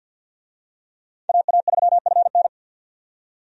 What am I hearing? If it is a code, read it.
Morse code, heard as a keyed tone, about 51 wpm, 700 Hz. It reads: AA3FN
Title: USA Call Signs - 5 Characters - 50wpm